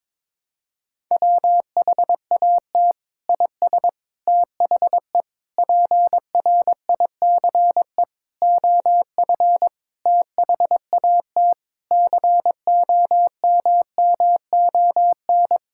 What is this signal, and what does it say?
Morse code, heard as a keyed tone, about 22 wpm, 700 Hz. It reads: WHAT IS THE PRICE OF THAT COMMON
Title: Sentences from Top 1000 Words - Review Part 5 - 22wpm